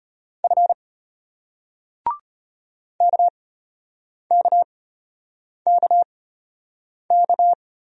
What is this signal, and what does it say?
Morse code, heard as a keyed tone, about 36 wpm, spaced as for 8 wpm, 700 Hz. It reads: F KKKK